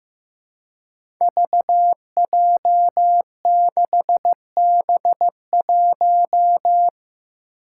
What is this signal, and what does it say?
Morse code, heard as a keyed tone, about 15 wpm, 700 Hz. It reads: VJ6B1
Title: Five Challenging Letter-Number Combinations 15wpm